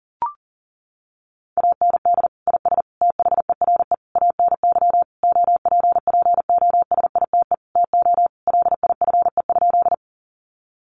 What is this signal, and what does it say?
Morse code, heard as a keyed tone, about 40 wpm, 700 Hz. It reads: AND IS THERE ANY OPPOSITE TO LIFE?
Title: Sentences from Top 1300 Words 40wpm